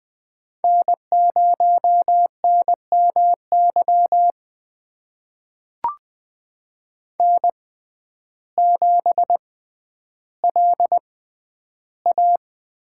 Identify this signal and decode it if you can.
Morse code, heard as a keyed tone, about 20 wpm, spaced as for 11 wpm, 700 Hz. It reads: N0NMY N7LA
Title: USA Call Signs - 5 Characters - 6x Character Spacing - 20wpm